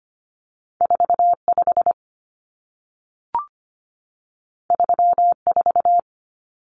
Morse code, heard as a keyed tone, about 25 wpm, 700 Hz. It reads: 45 34
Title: Numbers - 2-Digit 25wpm